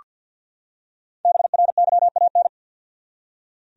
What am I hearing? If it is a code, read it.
Morse code, heard as a keyed tone, about 45 wpm, 700 Hz. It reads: BRYAN
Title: USA Common Names - Max 5 Letters - 50wpm